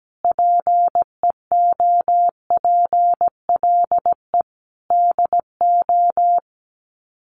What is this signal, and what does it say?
Morse code, heard as a keyed tone, about 17 wpm, 700 Hz. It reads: PEOPLE DO
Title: Sets of 5 Words 17wpm